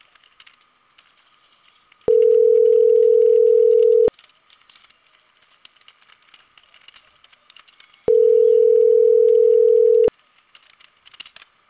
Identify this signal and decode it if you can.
Ringback tone (North America), 440 + 480 Hz.